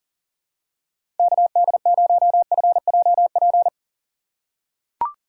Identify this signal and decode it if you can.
Morse code, heard as a keyed tone, about 40 wpm, 700 Hz. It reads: KD0FJP
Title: USA Call Signs - 6 Characters - 3x Character Spacing - 40wpm